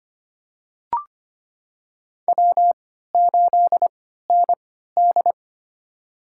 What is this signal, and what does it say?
Morse code, heard as a keyed tone, about 25 wpm, spaced as for 14 wpm, 700 Hz. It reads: W8ND
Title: USA Call Signs - 4 Characters - 3x Character Spacing - 25wpm